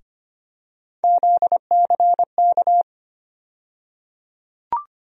Morse code, heard as a keyed tone, about 25 wpm, 700 Hz. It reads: ZCK
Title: USA Call Sign Postfixes - 3 Letters 25wpm